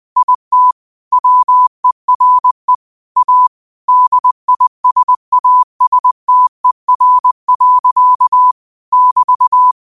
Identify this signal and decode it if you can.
Morse code, heard as a keyed tone, about 20 wpm, 1000 Hz. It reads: IT WERE A DISASTER. <BT>
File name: qotdfiles / 20wpm